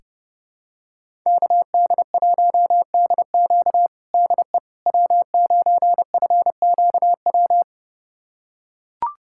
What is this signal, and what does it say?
Morse code, heard as a keyed tone, about 30 wpm, 700 Hz. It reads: KD1DQ DE W9FQW
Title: USA Callsign DE Callsign - 5 Characters - 30wpm